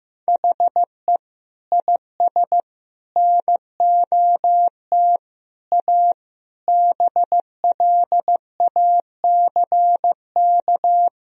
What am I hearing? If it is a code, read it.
Morse code, heard as a keyed tone, about 15 wpm, 700 Hz. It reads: HE IS NOT A BLACK
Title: Sentences from Top 1000 Words - Review Part 2 - 15wpm